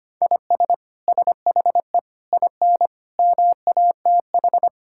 Morse code, heard as a keyed tone, about 25 wpm, 700 Hz. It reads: IS SHE IN MATH
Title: Sentences from Top 1000 Words - Review Part 6 - 25wpm